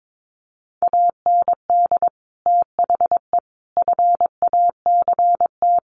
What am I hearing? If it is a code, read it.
Morse code, heard as a keyed tone, about 22 wpm, 700 Hz. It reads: AND THE FACT
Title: Sets of 4 Words 22wpm